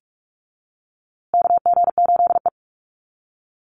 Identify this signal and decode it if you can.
Morse code, heard as a keyed tone, about 46 wpm, 690 Hz. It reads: KG8E
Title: USA Call Signs Repeated 3x 45wpm